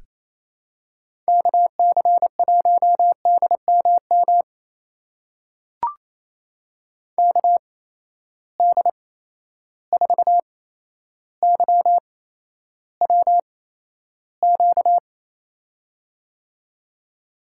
Morse code, heard as a keyed tone, about 28 wpm, spaced as for 12 wpm, 700 Hz. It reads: KC1DMM KD4YWQ
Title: USA Call Signs - 6 Characters - 8x Character Spacing - 28wpm